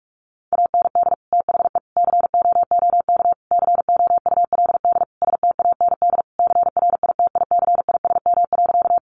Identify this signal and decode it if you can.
Morse code, heard as a keyed tone, about 44 wpm, 700 Hz. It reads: AND THE COOK COULD STAND CRITICISM.
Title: Sentences from Top 1300 Words 45wpm